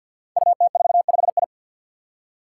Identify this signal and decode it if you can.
Morse code, heard as a keyed tone, about 48 wpm, 700 Hz. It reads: UT45I
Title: Five Challenging Letter-Number Combinations 50wpm